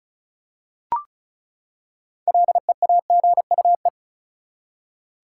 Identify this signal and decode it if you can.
Morse code, heard as a keyed tone, about 34 wpm, 700 Hz. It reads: LEAGUE
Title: Six Letter Words - 35wpm